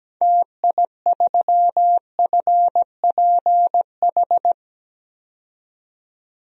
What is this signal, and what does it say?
Morse code, heard as a keyed tone, about 17 wpm, 700 Hz. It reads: TI3FPH
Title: DXCC-100 Call Signs - 6 Characters - 17wpm